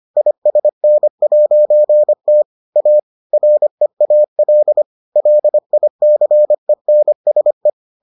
Morse code, heard as a keyed tone, about 25 wpm, 600 Hz. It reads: ISN'T A REAL LICENSE